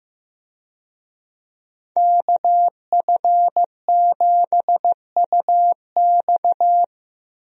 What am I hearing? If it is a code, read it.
Morse code, heard as a keyed tone, about 15 wpm, 700 Hz. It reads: KF7UX
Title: USA Call Signs - 5 Characters - Repeated 3x - 15wpm